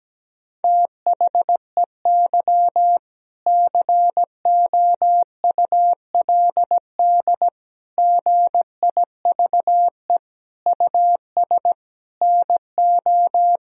Morse code, heard as a keyed tone, about 17 wpm, 700 Hz. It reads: THEY COULD GIVE US NO
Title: Sentences from Top 1000 Words - Review Part 3 - 17wpm